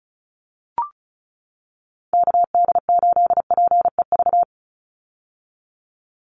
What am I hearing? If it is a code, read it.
Morse code, heard as a keyed tone, about 34 wpm, 700 Hz. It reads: KD8PEV
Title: USA Call Signs 35wpm